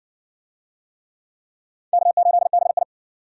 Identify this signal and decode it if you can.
Morse code, heard as a keyed tone, about 52 wpm, 700 Hz. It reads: K8BI